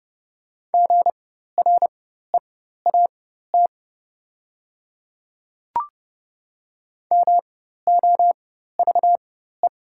Morse code, heard as a keyed tone, about 30 wpm, spaced as for 14 wpm, 700 Hz. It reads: GREAT MOVE
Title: 4x Character Spacing - Top 200 Words - 30wpm